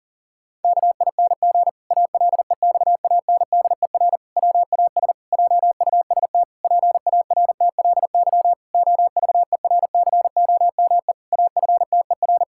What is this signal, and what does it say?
Morse code, heard as a keyed tone, about 40 wpm, 700 Hz. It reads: KING ALEXANDER WAS JUST PARTLY OVERCOME AFTER
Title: Pangram Challenge v2 - 40wpm